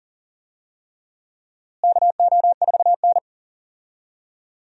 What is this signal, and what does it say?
Morse code, heard as a keyed tone, about 40 wpm, 700 Hz. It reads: KO4N
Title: USA Call Signs - 4 Characters - Repeated 3x - 40wpm